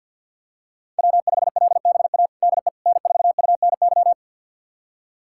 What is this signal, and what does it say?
Morse code, heard as a keyed tone, about 51 wpm, 700 Hz. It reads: W5LBA DE N4UNY
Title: USA Callsign DE Callsign - 5 Characters - 50wpm